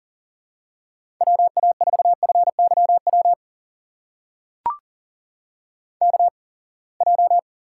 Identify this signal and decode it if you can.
Morse code, heard as a keyed tone, about 37 wpm, spaced as for 27 wpm, 700 Hz. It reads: WA4FYW KJ
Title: USA Call Signs - 6 Characters - 8x Character Spacing - 40wpm